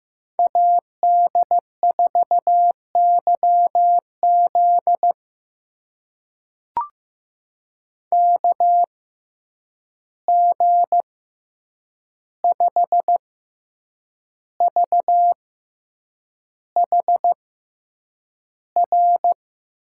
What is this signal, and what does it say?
Morse code, heard as a keyed tone, about 15 wpm, spaced as for 7 wpm, 700 Hz. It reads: AD4YZ KG5VHR
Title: USA Call Signs - 6x Character Spacing - 15wpm